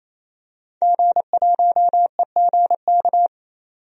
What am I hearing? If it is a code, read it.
Morse code, heard as a keyed tone, about 28 wpm, 700 Hz. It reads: G1EGK
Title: DXCC-100 Call Signs - 5 Characters - 28wpm